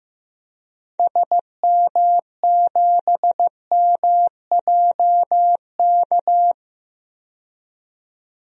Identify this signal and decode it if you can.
Morse code, heard as a keyed tone, about 15 wpm, 700 Hz. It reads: SM7MJK
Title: DXCC-100 Call Signs - 6 Characters - 15wpm